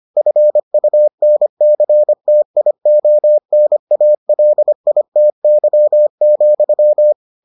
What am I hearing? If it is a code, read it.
Morse code, heard as a keyed tone, about 25 wpm, 600 Hz. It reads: FUNCTIONALITY,